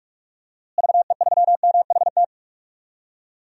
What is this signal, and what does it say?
Morse code, heard as a keyed tone, about 42 wpm, 700 Hz. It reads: VE3MHT
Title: DXCC-100 Call Signs - 6 Characters - Repeated 3x - 45wpm